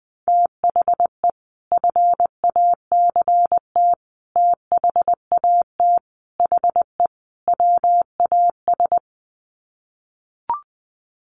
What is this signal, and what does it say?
Morse code, heard as a keyed tone, about 20 wpm, 700 Hz. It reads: THE FACT THAT HE WAS
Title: Sets of 5 Words 20wpm